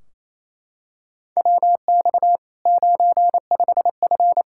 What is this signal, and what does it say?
Morse code, heard as a keyed tone, about 28 wpm, 700 Hz. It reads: WX 95F
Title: K1USN SST QSO Practice 28wpm